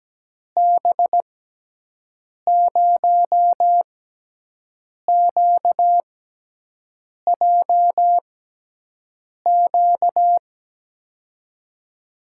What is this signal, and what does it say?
Morse code, heard as a keyed tone, about 17 wpm, spaced as for 6 wpm, 700 Hz. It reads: B0QJQ
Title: USA Call Signs - 6x Character Spacing - 17wpm